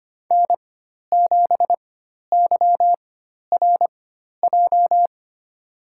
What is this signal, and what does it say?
Morse code, heard as a keyed tone, about 25 wpm, spaced as for 12 wpm, 700 Hz. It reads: N7YRJ
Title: USA Call Signs - 5 Characters - 4x Character Spacing - 25wpm